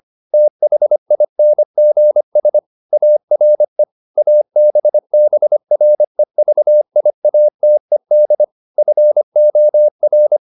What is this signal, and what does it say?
Morse code, heard as a keyed tone, about 25 wpm, 600 Hz. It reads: THINGS ARE ABBREVIATED FOR
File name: SS319 25